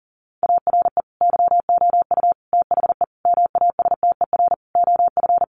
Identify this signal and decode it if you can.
Morse code, heard as a keyed tone, about 40 wpm, 700 Hz. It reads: ARE YOU THE MASTER OF
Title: Sentences from Top 1300 Words 40wpm